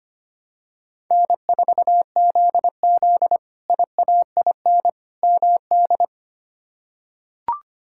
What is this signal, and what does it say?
Morse code, heard as a keyed tone, about 25 wpm, 700 Hz. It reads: N4ZZ IAIN MD
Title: K1USN SST QSO Practice - 25wpm